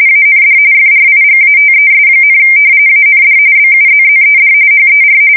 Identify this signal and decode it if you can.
RTTY, about 45.45 baud. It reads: AND FRIGATES, AND BUZZED HER
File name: fsk170 / battleshipnj